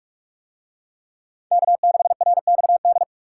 Morse code, heard as a keyed tone, about 41 wpm, 700 Hz. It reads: K6RXD